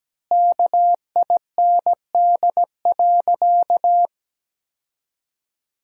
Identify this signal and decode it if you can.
Morse code, heard as a keyed tone, about 17 wpm, 700 Hz. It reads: KIND.
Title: 2x Word-Spacing - Sentences from Top 200 Words - 17wpm